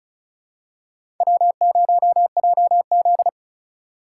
Morse code, heard as a keyed tone, about 36 wpm, 690 Hz. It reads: W0JZ
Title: USA Call Signs - 4 Characters - 3x Character Spacing - 35wpm